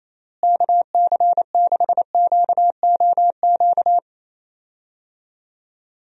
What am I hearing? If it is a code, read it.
Morse code, heard as a keyed tone, about 27 wpm, 700 Hz. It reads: KC6QOQ
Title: USA Call Signs - 6 Characters - Repeated 3x - 28wpm